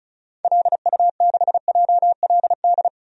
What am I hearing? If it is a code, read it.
Morse code, heard as a keyed tone, about 34 wpm, 700 Hz. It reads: LU6JLD
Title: DXCC-100 Call Signs - 6 Characters - Repeated 3x - 35wpm